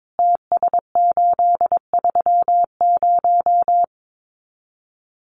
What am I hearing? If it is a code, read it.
Morse code, heard as a keyed tone, about 22 wpm, 700 Hz. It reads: TS830
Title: Common QSO Phrases 22wpm